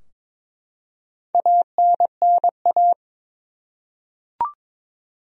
Morse code, heard as a keyed tone, about 22 wpm, 700 Hz. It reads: ANNA
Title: USA Common Names - Max 5 Letters - 22wpm